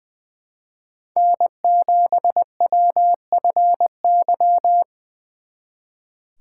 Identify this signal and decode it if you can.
Morse code, heard as a keyed tone, about 25 wpm, 700 Hz. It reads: N7WFY
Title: Speed Racing - USA 1x3 Call Signs 20wpm